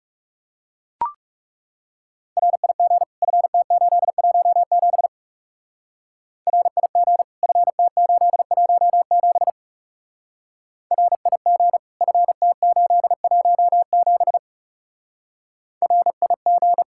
Morse code, heard as a keyed tone, about 36 wpm, 700 Hz. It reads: RIG FT817 RIG FT817 RIG FT817 RIG
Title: Speed-Racing Common QSO Phrases - 30wpm